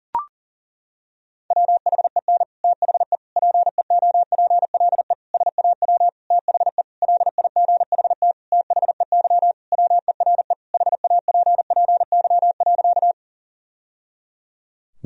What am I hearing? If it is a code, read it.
Morse code, heard as a keyed tone, about 40 wpm, 700 Hz. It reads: WHEN THE PEOPLE SAW THE LIGHT THEY WERE HAPPY.
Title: Sentences from Top 1000 Words - Review Part 1 - 40wpm